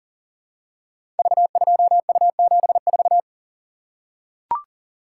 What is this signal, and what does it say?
Morse code, heard as a keyed tone, about 43 wpm, 690 Hz. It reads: V2U74
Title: Code Groups - 5 Letter:Numbers - 40wpm